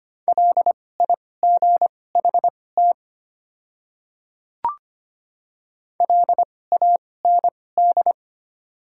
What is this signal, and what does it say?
Morse code, heard as a keyed tone, about 25 wpm, spaced as for 18 wpm, 690 Hz. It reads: LIGHT LAND
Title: 2x Character Spacing - Top 200 Words - 25wpm